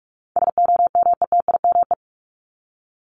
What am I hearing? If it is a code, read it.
Morse code, heard as a keyed tone, about 44 wpm, 700 Hz. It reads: SOMETIME
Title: ES:S Suffix - Top 100 Words with Base Words 45wpm